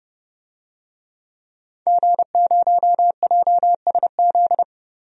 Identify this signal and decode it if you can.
Morse code, heard as a keyed tone, about 32 wpm, 700 Hz. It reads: G0JSZ
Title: DXCC-100 Call Signs - 5 Characters - Repeated 2x - 30wpm